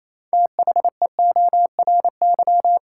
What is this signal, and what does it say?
Morse code, heard as a keyed tone, about 28 wpm, 700 Hz. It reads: THEORY
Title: Six Letter Words - 28wpm